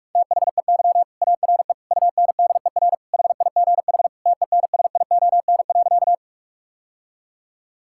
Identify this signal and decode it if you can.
Morse code, heard as a keyed tone, about 47 wpm, 700 Hz. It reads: THEY ARE UNDER HIGH TENSION.
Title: Sentences from Top 1600 Words 45wpm